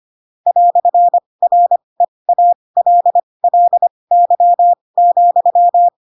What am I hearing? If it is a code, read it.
Morse code, heard as a keyed tone, about 25 wpm, spaced as for 20 wpm, 700 Hz. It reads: "REALLY,
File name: Pinocchio / Pinocchio ch7 15 25 ik1hge